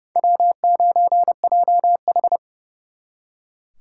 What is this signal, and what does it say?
Morse code, heard as a keyed tone, about 40 wpm, 700 Hz. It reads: W9JH